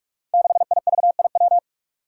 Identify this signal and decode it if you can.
Morse code, heard as a keyed tone, about 44 wpm, 700 Hz. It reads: 6IVIW
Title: Five Challenging Letter-Number Combinations 45wpm